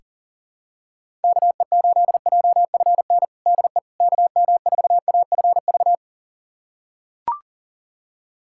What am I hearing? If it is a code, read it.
Morse code, heard as a keyed tone, about 40 wpm, 700 Hz. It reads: KE8JFN DE KM4AFV
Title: USA Callsign DE Callsign - 6 Characters - 40wpm